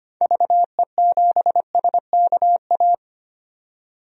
Morse code, heard as a keyed tone, about 25 wpm, 700 Hz. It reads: VE7SKA
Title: DXCC-100 Call Signs - 6 Characters - 25wpm